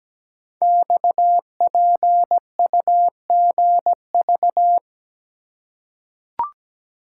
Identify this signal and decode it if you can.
Morse code, heard as a keyed tone, about 17 wpm, 700 Hz. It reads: XPUGV
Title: Code Groups - 5 Letters - 17wpm